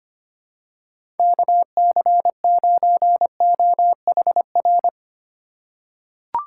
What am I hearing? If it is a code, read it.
Morse code, heard as a keyed tone, about 25 wpm, 700 Hz. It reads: KC9OHR
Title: USA Call Signs - 6 Characters - 4x Character Spacing - 25wpm